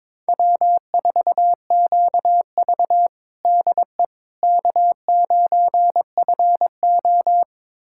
Morse code, heard as a keyed tone, about 22 wpm, 700 Hz. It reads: W4QV DE K9FO
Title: USA Callsign DE Callsign - 4 Characters - 22wpm